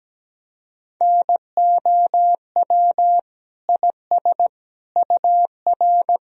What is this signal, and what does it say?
Morse code, heard as a keyed tone, about 17 wpm, 700 Hz. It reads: NOW IS UR
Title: Sentences from Top 100 Words and Abbreviations 17wpm